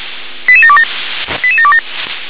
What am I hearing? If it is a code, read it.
Five-tone selective call (ZVEI), heard as quick runs of five tones, tones of about 70 ms. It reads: 88727 88727